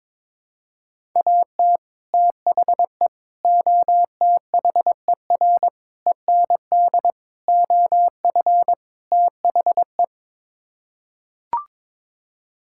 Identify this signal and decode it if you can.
Morse code, heard as a keyed tone, about 22 wpm, 700 Hz. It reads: AT THE OTHER END OF THE
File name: practice / 22wpm / Sets of 6 Words - 22wpm